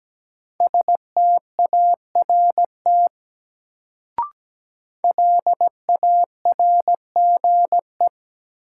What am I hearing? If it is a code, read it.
Morse code, heard as a keyed tone, about 17 wpm, 700 Hz. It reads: START LARGE
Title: Sets of 4 Words - New Word Drill - 17wpm